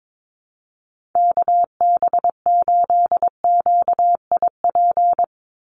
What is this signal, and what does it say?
Morse code, heard as a keyed tone, about 22 wpm, 700 Hz. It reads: KB8QIP
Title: USA Call Signs 22wpm